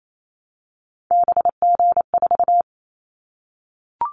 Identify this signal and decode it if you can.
Morse code, heard as a keyed tone, about 28 wpm, 700 Hz. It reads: BG4
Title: Callsign Prefixes for Top 100 DXCC - 28wpm